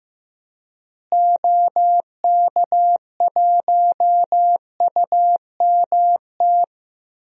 Morse code, heard as a keyed tone, about 15 wpm, 700 Hz. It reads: OK1UMT
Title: DXCC-100 Call Signs - 6 Characters - Repeated 3x - 15wpm